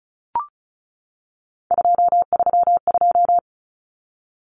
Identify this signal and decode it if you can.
Morse code, heard as a keyed tone, about 34 wpm, 700 Hz. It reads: 232
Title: Numbers - 3-Digit 35wpm